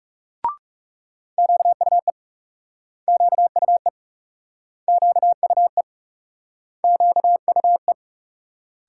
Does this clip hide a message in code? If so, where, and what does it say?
Yes: Morse code from 1.4 to 7.9 s, heard as a keyed tone, about 39 wpm, 700 Hz. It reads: QUE QUE QUE QUE